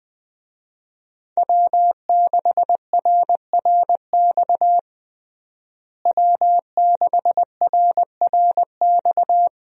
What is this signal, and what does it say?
Morse code, heard as a keyed tone, about 20 wpm, 700 Hz. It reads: W6RRX W6RRX
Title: USA Call Signs - 5 Characters - Repeated 2x - 20wpm